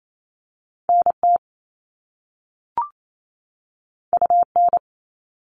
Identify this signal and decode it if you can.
Morse code, heard as a keyed tone, about 28 wpm, 700 Hz. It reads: NT UN
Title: All Cut Numbers - 2-Digit 28wpm